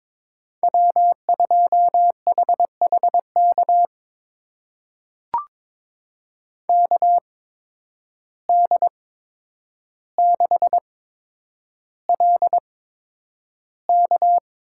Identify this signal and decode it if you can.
Morse code, heard as a keyed tone, about 22 wpm, spaced as for 9 wpm, 700 Hz. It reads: W2HHK KD6LK
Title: USA Call Signs - 5 Characters - 8x Character Spacing - 22wpm